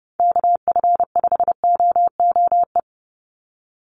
Morse code, heard as a keyed tone, about 30 wpm, 700 Hz. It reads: KF5OOE
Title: USA Call Signs 30wpm